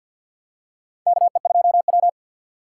Morse code, heard as a keyed tone, about 53 wpm, 700 Hz. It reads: KE2W
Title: USA Call Signs - 4 Characters - 2x Character Spacing - 50wpm